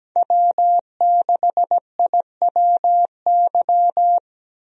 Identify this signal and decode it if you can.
Morse code, heard as a keyed tone, about 17 wpm, 700 Hz. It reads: W6IWY
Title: USA Call Signs - 5 Characters - Repeated 2x - 17wpm